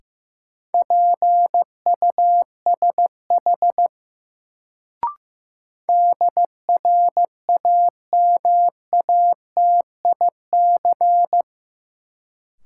Morse code, heard as a keyed tone, about 15 wpm, 700 Hz. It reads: PUSH DRAMATIC
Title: Sentences from Top 1300 Words - New Word Drill - 15wpm